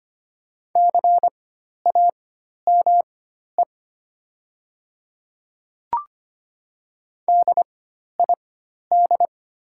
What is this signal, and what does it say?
Morse code, heard as a keyed tone, about 25 wpm, spaced as for 12 wpm, 700 Hz. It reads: CAME DID
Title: 4x Character Spacing - Top 200 Words - 25wpm